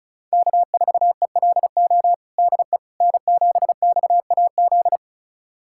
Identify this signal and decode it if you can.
Morse code, heard as a keyed tone, about 36 wpm, 700 Hz. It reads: K4ELO DE N7XAZ
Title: USA Callsign DE Callsign 35wpm